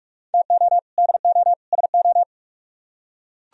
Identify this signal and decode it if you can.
Morse code, heard as a keyed tone, about 44 wpm, 700 Hz. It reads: TO DO SO